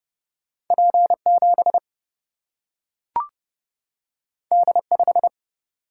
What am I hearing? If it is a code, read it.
Morse code, heard as a keyed tone, about 29 wpm, 700 Hz. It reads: P7 D5
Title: Single-Letter Single-Number 30wpm